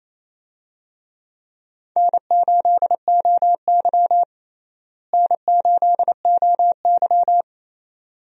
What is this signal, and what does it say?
Morse code, heard as a keyed tone, about 28 wpm, 700 Hz. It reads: N8OY N8OY
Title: USA Call Signs - 4 Characters - Repeated 2x - 28wpm